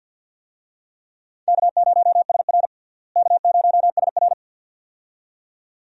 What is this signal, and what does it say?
Morse code, heard as a keyed tone, about 47 wpm, 690 Hz. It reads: K0SR K0SR
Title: USA Call Signs - 4 Characters - Repeated 2x - 50wpm